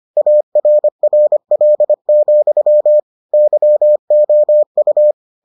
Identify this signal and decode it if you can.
Morse code, heard as a keyed tone, about 25 wpm, 600 Hz. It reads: ARRL, YOU